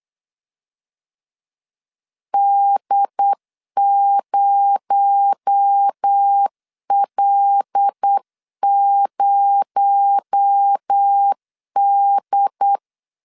Morse code, heard as a keyed tone, about 8 wpm, 800 Hz. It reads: D0L0D
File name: LEZIONI / 6 PKX0 / 6 PKX0 8wpm 3341